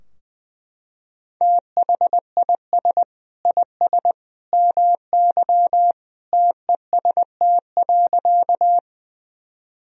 Morse code, heard as a keyed tone, about 21 wpm, 700 Hz. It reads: THIS IS MY TEST.
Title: Sentences from Top 1000 Words - Review Part 3 - 20wpm